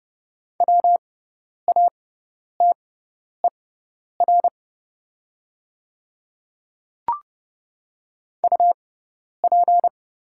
Morse code, heard as a keyed tone, about 29 wpm, spaced as for 10 wpm, 690 Hz. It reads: WATER UP
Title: 6x Character Spacing - Top 200 Words - 30wpm